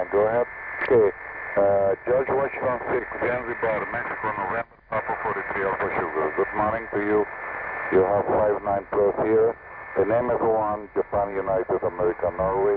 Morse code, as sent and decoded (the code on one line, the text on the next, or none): none